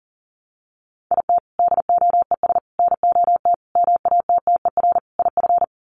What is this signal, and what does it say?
Morse code, heard as a keyed tone, about 40 wpm, 700 Hz. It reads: IT DOES NOT MATTER IF
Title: Sets of 5 Words Encore 40wpm